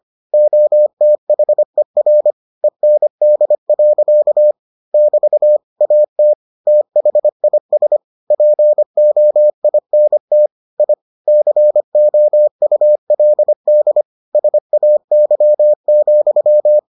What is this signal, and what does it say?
Morse code, heard as a keyed tone, about 25 wpm, 600 Hz. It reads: OTHER END. <BT> AT THIS POINT I COULD SAY,